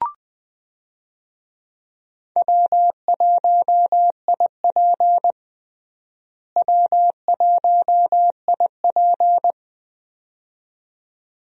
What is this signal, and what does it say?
Morse code, heard as a keyed tone, about 20 wpm, 700 Hz. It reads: W1IP W1IP
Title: USA Call Signs - 4 Characters - Repeated 2x - 20wpm